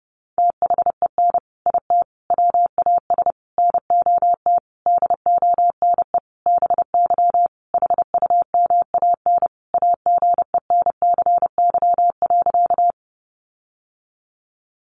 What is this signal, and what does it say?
Morse code, heard as a keyed tone, about 29 wpm, 700 Hz. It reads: THEN IT WAS NOT DONE BY HUMAN AGENCY.
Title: Sentences from Top 1300 Words 30wpm